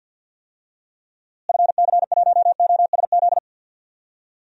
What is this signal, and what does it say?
Morse code, heard as a keyed tone, about 50 wpm, 700 Hz. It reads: FC1OSZ
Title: DXCC-100 Call Signs - 6 Characters - Repeated 3x - 50wpm